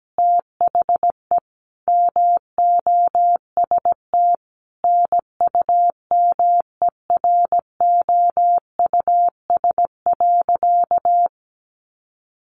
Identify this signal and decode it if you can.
Morse code, heard as a keyed tone, about 17 wpm, 700 Hz. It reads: THE MOST NUMEROUS.
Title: Sentences from Top 1400 Words 17wpm